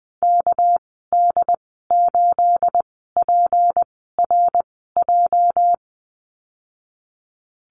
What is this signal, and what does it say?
Morse code, heard as a keyed tone, about 20 wpm, spaced as for 15 wpm, 700 Hz. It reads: KD8PRJ
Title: 238 Lesson 38g - Letter q - Call Signs 20wpm